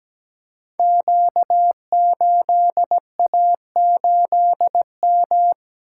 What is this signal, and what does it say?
Morse code, heard as a keyed tone, about 17 wpm, 700 Hz. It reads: Q8A8M
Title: Code Groups - 5 Letter:Numbers - 17wpm